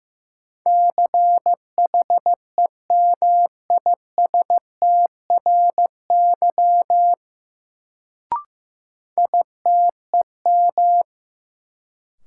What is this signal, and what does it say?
Morse code, heard as a keyed tone, about 16 wpm, 700 Hz. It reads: CHEMISTRY ITEM
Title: Sentences from Top 1000 Words - Review Part 6 - New Word Drill - 15wpm